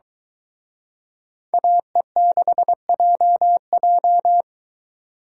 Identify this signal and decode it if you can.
Morse code, heard as a keyed tone, about 23 wpm, 700 Hz. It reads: AE6JJ